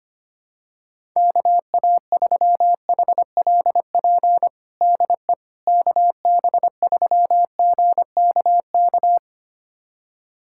Morse code, heard as a keyed tone, about 25 wpm, 700 Hz. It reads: KA3HLP DE KB3GKK
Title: USA Callsign DE Callsign 25wpm